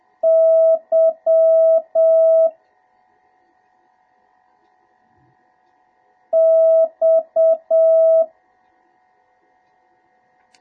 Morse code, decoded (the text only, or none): Y X